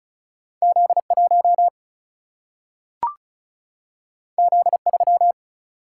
Morse code, heard as a keyed tone, about 36 wpm, 700 Hz. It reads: Z1 Z3
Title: Single-Letter Single-Number 35wpm